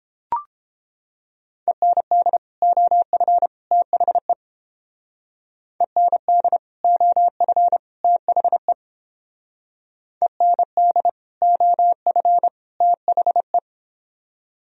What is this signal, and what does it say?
Morse code, heard as a keyed tone, about 29 wpm, 700 Hz. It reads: END OF THE END OF THE END OF THE